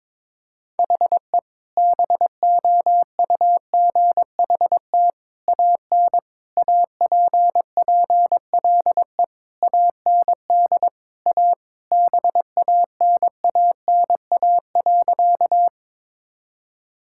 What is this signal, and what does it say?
Morse code, heard as a keyed tone, about 22 wpm, 700 Hz. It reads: HE BOUGHT AN APPLE AND A BANANA.
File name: practice / 22wpm / Sentences with 2-Word Phrases - 22wpm